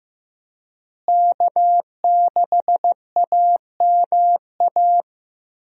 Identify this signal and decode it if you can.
Morse code, heard as a keyed tone, about 15 wpm, 700 Hz. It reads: K6AMA
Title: USA Call Signs - 5 Characters - Repeated 2x - 15wpm